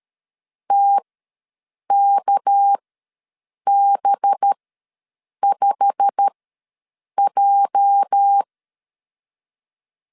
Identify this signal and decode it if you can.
Morse code, heard as a keyed tone, about 13 wpm, spaced as for 7 wpm, 800 Hz. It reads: TKB5J
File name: LEZIONI / 10 SIGNS / 10A SLASH ws 12wpm 3443